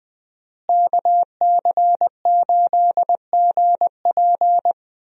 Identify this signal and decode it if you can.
Morse code, heard as a keyed tone, about 21 wpm, 700 Hz. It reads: KC8GP
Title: USA Call Signs - 5 Characters - Repeated 3x - 20wpm